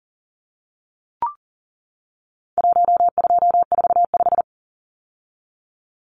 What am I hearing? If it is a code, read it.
Morse code, heard as a keyed tone, about 40 wpm, 700 Hz. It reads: 1245
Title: Numbers - Contest Style - 4-Digit 40wpm